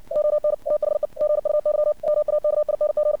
Morse code, heard as a keyed tone, about 56 wpm, 600 Hz. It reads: ON THE GUY MAKING
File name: archived / CRSnet